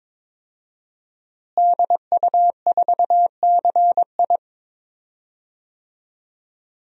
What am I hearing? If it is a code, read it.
Morse code, heard as a keyed tone, about 22 wpm, 700 Hz. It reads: DU4CI
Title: DXCC-100 Call Signs - 5 Characters - Repeated 2x - 22wpm